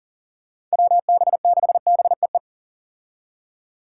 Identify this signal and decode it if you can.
Morse code, heard as a keyed tone, about 40 wpm, 690 Hz. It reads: WB6BEE
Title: K1USN SST QSO Practice - 40wpm